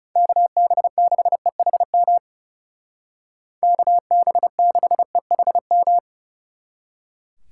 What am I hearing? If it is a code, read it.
Morse code, heard as a keyed tone, about 38 wpm, 700 Hz. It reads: KB6EHM KB6EHM